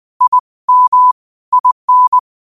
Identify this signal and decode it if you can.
Morse code, heard as a keyed tone, about 20 wpm, 1000 Hz. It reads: I M IN
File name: qotdfiles / 20wpm